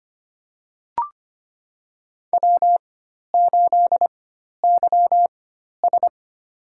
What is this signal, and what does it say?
Morse code, heard as a keyed tone, about 25 wpm, spaced as for 12 wpm, 700 Hz. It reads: W8YS
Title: USA Call Signs - 4 Characters - 4x Character Spacing - 25wpm